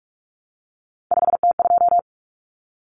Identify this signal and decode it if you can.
Morse code, heard as a keyed tone, about 44 wpm, 700 Hz. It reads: 5T2
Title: Common Cut Numbers - 3-Digit 45wpm